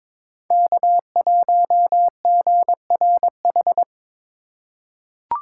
Morse code, heard as a keyed tone, about 22 wpm, 700 Hz. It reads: K1GRH
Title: USA Call Signs - 5 Characters - 3x Character Spacing - 22wpm